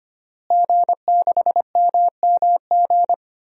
Morse code, heard as a keyed tone, about 25 wpm, 700 Hz. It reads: G6MMG
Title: DXCC-100 Call Signs - 5 Characters - Repeated 3x - 25wpm